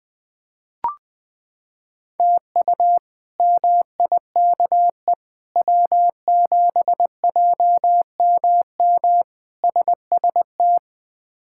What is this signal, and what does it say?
Morse code, heard as a keyed tone, about 20 wpm, 700 Hz. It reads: TU MIKE W7JMM SST
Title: K1USN SST QSO Practice - 20wpm